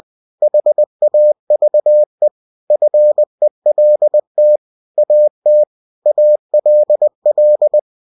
Morse code, heard as a keyed tone, about 20 wpm, 600 Hz. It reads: HAVE FELT AT ALL
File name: cw-books / winnie-the-pooh / winnie-20wpm-page-0026